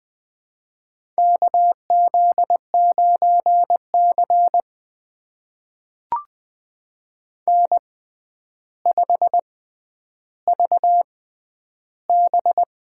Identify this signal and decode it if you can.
Morse code, heard as a keyed tone, about 20 wpm, spaced as for 10 wpm, 700 Hz. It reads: KZ9C N5VB
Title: USA Call Signs - 4 Characters - 6x Character Spacing - 20wpm